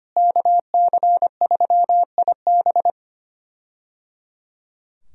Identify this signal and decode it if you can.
Morse code, heard as a keyed tone, about 25 wpm, 700 Hz. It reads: KC3IB
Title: USA Call Signs - 5 Characters - Repeated 3x - 25wpm